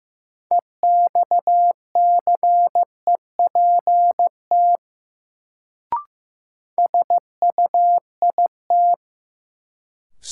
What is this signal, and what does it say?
Morse code, heard as a keyed tone, about 15 wpm, 700 Hz. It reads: EXCEPT SUIT
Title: Sentences from Top 1600 Words - New Word Drill Alt - 15wpm